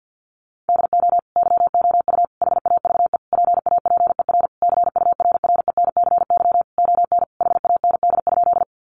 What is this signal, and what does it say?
Morse code, heard as a keyed tone, about 50 wpm, 700 Hz. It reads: DO YOU HAVE PAPER CURRENCY ON HAND?
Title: Sentences from Top 1500 Words 50wpm